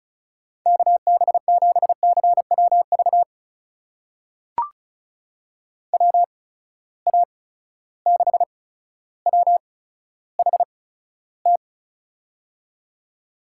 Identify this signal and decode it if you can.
Morse code, heard as a keyed tone, about 36 wpm, spaced as for 15 wpm, 700 Hz. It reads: KB7CWV WA6WHT
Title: USA Call Signs - 6 Characters - 8x Character Spacing - 35wpm